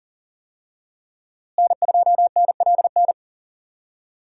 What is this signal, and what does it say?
Morse code, heard as a keyed tone, about 40 wpm, 700 Hz. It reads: N2NLN
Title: USA Call Signs - 5 Characters - Repeated 2x - 40wpm